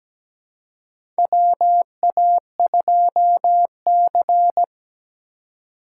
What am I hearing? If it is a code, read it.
Morse code, heard as a keyed tone, about 17 wpm, 700 Hz. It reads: WA2C
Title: USA Call Signs - 4 Characters - 2x Character Spacing - 17wpm